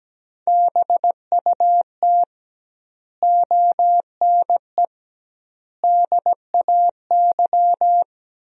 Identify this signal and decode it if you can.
Morse code, heard as a keyed tone, about 17 wpm, 700 Hz. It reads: BUT ONE DAY
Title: 2x Word-Spacing - Sentences from Top 400 Words - 17wpm